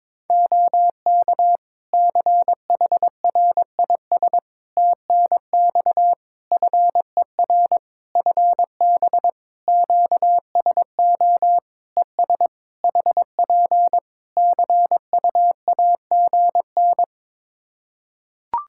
Morse code, heard as a keyed tone, about 22 wpm, 700 Hz. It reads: OK CHRIS TNX FER FB QSO ES HP CUAGN
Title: Practice QSO - Exchange 4 - Ending - 22wpm